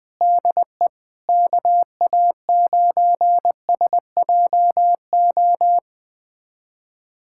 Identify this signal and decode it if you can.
Morse code, heard as a keyed tone, about 20 wpm, 700 Hz. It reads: DE KA9SJO
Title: USA Callsign DE Callsign - 6 Characters - 20wpm